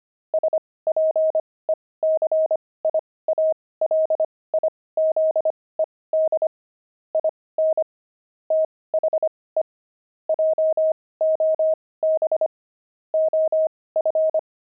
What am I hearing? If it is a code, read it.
Morse code, heard as a keyed tone, about 25 wpm, spaced as for 18 wpm, 630 Hz. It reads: SPECIALIZED IN THE JOB OF
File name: pangram-fs-18-cs-25-toys